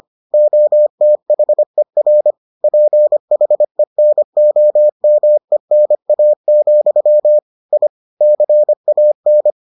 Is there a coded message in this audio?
Yes: Morse code from 0.3 to 9.5 s, heard as a keyed tone, about 25 wpm, 600 Hz. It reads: OTHER PHENOMENA, I CAN